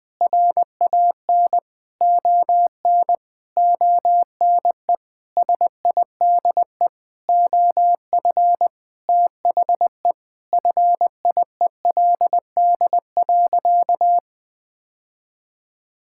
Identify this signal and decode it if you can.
Morse code, heard as a keyed tone, about 20 wpm, 700 Hz. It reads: RAN ON ONE SIDE OF THE FIELD.
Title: Sentences with 5-Word Phrases - 20wpm